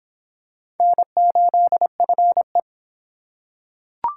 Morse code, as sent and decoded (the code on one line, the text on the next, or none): -. ---.. ..-. .
N8FE